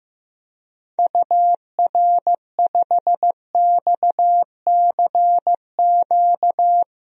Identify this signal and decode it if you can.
Morse code, heard as a keyed tone, about 15 wpm, 700 Hz. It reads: UR5XCQ
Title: DXCC-100 Call Signs - 6 Characters - 15wpm